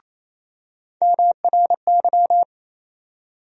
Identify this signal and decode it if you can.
Morse code, heard as a keyed tone, about 28 wpm, 700 Hz. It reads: MRY
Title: USA Call Sign Postfixes - 3 Letters 28wpm